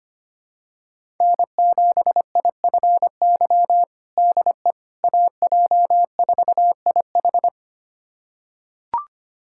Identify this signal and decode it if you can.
Morse code, heard as a keyed tone, about 25 wpm, 700 Hz. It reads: N7IFY DE AJ4IH
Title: USA Callsign DE Callsign - 5 Characters - 25wpm